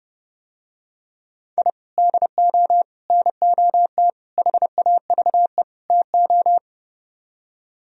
Morse code, heard as a keyed tone, about 30 wpm, 700 Hz. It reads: I DO NOT HAVE TO